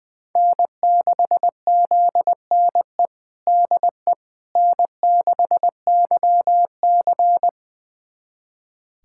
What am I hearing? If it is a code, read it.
Morse code, heard as a keyed tone, about 20 wpm, 700 Hz. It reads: N6ZNE DE N6YC
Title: USA Callsign DE Callsign 20wpm